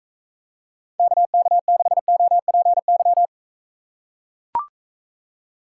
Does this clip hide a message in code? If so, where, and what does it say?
Yes: Morse code from 1.0 to 3.3 s, heard as a keyed tone, about 45 wpm, 700 Hz. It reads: KK6OPY